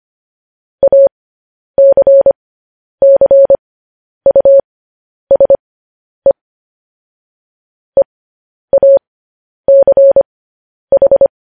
Morse code, heard as a keyed tone, about 25 wpm, spaced as for 10 wpm, 550 Hz. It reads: ACCUSE EACH
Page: News Headlines in Morse Code at 10 WPM